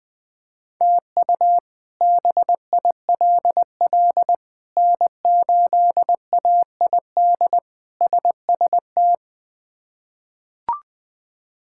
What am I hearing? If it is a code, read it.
Morse code, heard as a keyed tone, about 21 wpm, 700 Hz. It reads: TU BILL N8AID SST
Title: K1USN SST QSO Practice - 20wpm